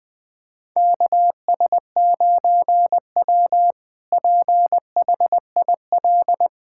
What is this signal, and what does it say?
Morse code, heard as a keyed tone, about 20 wpm, 700 Hz. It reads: KS9W PHIL
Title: K1USN SST QSO Practice - 20wpm